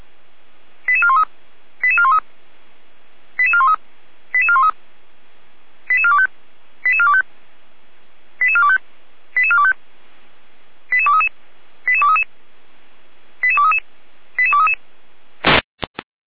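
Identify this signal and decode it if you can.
Five-tone selective call (ZVEI), heard as quick runs of five tones, tones of about 70 ms. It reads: 88413 88413 88413 88413 88425 88425 88425 88425 88233 88233 88233 88233
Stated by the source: ZVEI five-tone selective call 88393, received by SDR.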